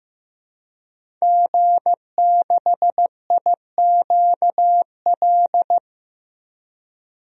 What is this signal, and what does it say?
Morse code, heard as a keyed tone, about 15 wpm, 700 Hz. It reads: G6IQL
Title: DXCC-100 Call Signs - 5 Characters - Repeated 2x - 15wpm